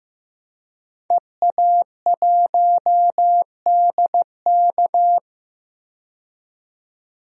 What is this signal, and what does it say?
Morse code, heard as a keyed tone, about 15 wpm, 700 Hz. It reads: EA1DK
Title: DXCC-100 Call Signs - 5 Characters - 15wpm